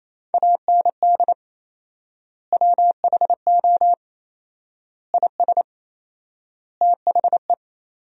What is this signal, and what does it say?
Morse code, heard as a keyed tone, about 28 wpm, 700 Hz. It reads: AND WHO IS THE
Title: 4x Word-Spacing - Sentences from Top 200 Words - 28wpm